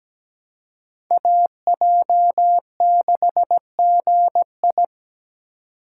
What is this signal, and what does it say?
Morse code, heard as a keyed tone, about 17 wpm, 700 Hz. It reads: AJ6GI
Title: USA Call Signs - 4x Character Spacing - 17wpm